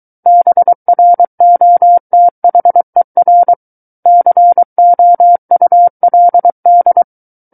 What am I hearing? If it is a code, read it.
Morse code, heard as a keyed tone, about 23 wpm, 700 Hz. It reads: BROTHER COULD
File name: Book One / 14-Chapter 14 - 23 WPM